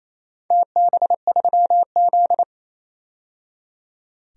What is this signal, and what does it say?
Morse code, heard as a keyed tone, about 28 wpm, 700 Hz. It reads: TB3Z
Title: DXCC-100 Call Signs - 4 Characters - 28wpm